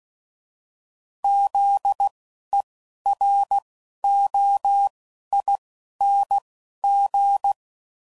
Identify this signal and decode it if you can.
Morse code, heard as a keyed tone, about 16 wpm, 800 Hz. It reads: Z E R O I N G
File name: Suffixes / Special INGs / Single INGs